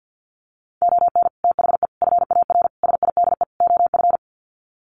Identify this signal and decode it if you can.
Morse code, heard as a keyed tone, about 50 wpm, 700 Hz. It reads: ON THE FAR SIDE OF
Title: Sets of 5 Words 50wpm